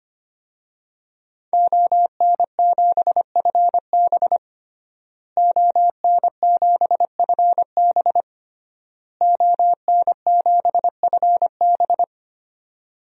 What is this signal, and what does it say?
Morse code, heard as a keyed tone, about 25 wpm, 700 Hz. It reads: ON7FB ON7FB ON7FB
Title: DXCC-100 Call Signs - 5 Characters - Repeated 3x - 25wpm